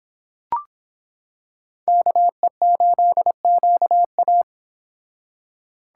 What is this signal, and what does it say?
Morse code, heard as a keyed tone, about 26 wpm, 700 Hz. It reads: KE8QA